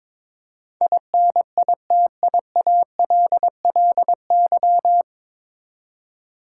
Morse code, heard as a keyed tone, about 22 wpm, 700 Hz. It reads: INITIALLY
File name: 9 Letter Words - First 3 Letters Given - 22wpm